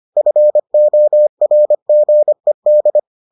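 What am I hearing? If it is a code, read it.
Morse code, heard as a keyed tone, about 25 wpm, 600 Hz. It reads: FORGED
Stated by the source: Morse edition of the Foundations of Amateur Radio episodes.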